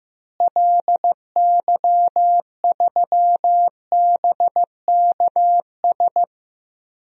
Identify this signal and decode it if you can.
Morse code, heard as a keyed tone, about 15 wpm, 700 Hz. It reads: LY3BKS
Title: DXCC-100 Call Signs - 6 Characters - Repeated 3x - 15wpm